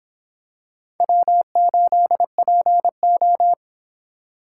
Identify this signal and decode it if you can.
Morse code, heard as a keyed tone, about 25 wpm, 700 Hz. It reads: W8PO